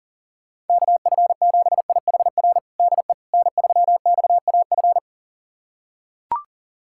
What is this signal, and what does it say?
Morse code, heard as a keyed tone, about 37 wpm, 700 Hz. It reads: KF7IHR DE N3XAF
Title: USA Callsign DE Callsign 40wpm